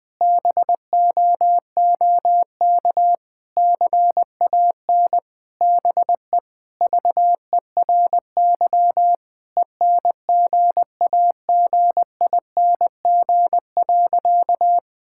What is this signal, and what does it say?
Morse code, heard as a keyed tone, about 21 wpm, 700 Hz. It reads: BOOK CAN BE VERY ENGAGING.
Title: Sentences with 2-Word Phrases - 20wpm